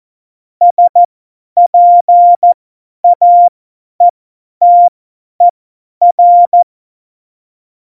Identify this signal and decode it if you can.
Morse code, heard as a keyed tone, about 14 wpm, spaced as for 10 wpm, 700 Hz. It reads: SPAETER
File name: Morsekurs / Witze / Witz 22. Dezember 2020, 50 F70